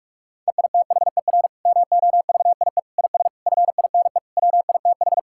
Morse code, heard as a keyed tone, about 46 wpm, 700 Hz. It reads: EITHER MOVIE IS FINE WITH
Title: Sentences from Top 1000 Words - Review Part 5 - 45wpm